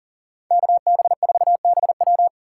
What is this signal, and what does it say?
Morse code, heard as a keyed tone, about 52 wpm, 700 Hz. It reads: KB4BW